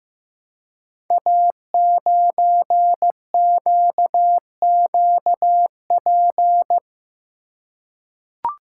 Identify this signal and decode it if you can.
Morse code, heard as a keyed tone, about 15 wpm, 700 Hz. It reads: A9QQP
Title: Code Groups - 5 Letter:Numbers - 15wpm